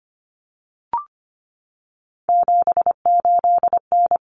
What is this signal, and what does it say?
Morse code, heard as a keyed tone, about 25 wpm, 700 Hz. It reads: 78N
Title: Common Cut Numbers - 3-Digit 25wpm